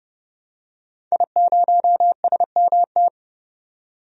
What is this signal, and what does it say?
Morse code, heard as a keyed tone, about 30 wpm, 700 Hz. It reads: I0SMT
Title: DXCC-100 Call Signs - 5 Characters - 30wpm